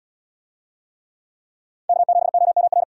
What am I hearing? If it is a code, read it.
Morse code, heard as a keyed tone, about 73 wpm, 700 Hz. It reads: K7OMR